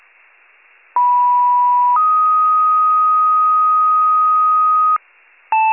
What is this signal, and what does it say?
Two-tone page, heard as a one-second tone, then a three-second tone, about 980 then 1250 Hz. It is heard from 1.0 to 5.0 s.